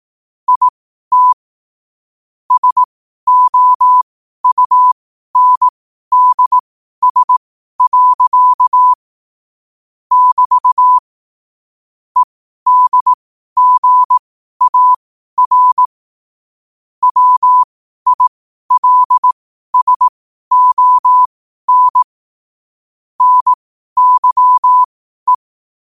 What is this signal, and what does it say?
Morse code, heard as a keyed tone, about 18 wpm, spaced as for 12 wpm, 1000 Hz. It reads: IT SOUNDS. <BT> EDGAR WILSON NYE
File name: qotdfiles / 12wpm